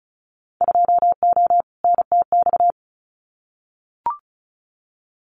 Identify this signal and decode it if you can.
Morse code, heard as a keyed tone, about 36 wpm, 700 Hz. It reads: 2O NTX
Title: Winter Field Day Exchanges 35wpm